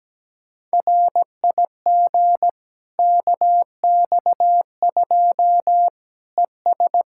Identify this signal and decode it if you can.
Morse code, heard as a keyed tone, about 17 wpm, 700 Hz. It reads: RIG KX2 ES
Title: K1USN SST QSO Practice 17wpm